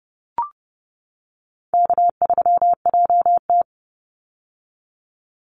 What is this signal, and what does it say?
Morse code, heard as a keyed tone, about 30 wpm, 700 Hz. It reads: K3JT
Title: Easy Call Signs - 30wpm